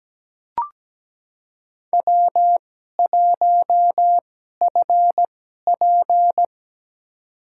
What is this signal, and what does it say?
Morse code, heard as a keyed tone, about 17 wpm, spaced as for 12 wpm, 700 Hz. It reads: W1FP
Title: USA Call Signs - 2x Character Spacing - 17wpm